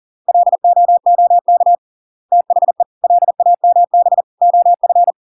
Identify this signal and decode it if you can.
Morse code, heard as a keyed tone, about 40 wpm, 700 Hz. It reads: LOOK THE LAMB OF